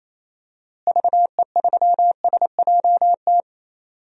Morse code, heard as a keyed tone, about 27 wpm, 700 Hz. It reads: VE3SJT
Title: DXCC-100 Call Signs - 6 Characters - Repeated 3x - 28wpm